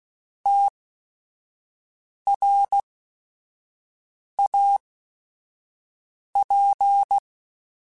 Morse code, heard as a keyed tone, about 16 wpm, spaced as for 5 wpm, 800 Hz. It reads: TRAP
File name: QSOs / QSOs-mp3s / WD0FBT 1@5